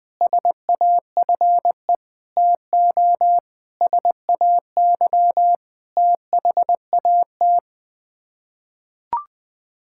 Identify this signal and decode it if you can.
Morse code, heard as a keyed tone, about 20 wpm, 700 Hz. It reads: SAFE TO SAY THAT
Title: Sets of 8 Words - 20wpm